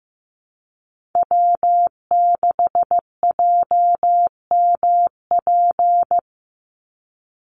Morse code, heard as a keyed tone, about 15 wpm, 700 Hz. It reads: W6JMP
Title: NA SOTA Chaser Callsigns 15wpm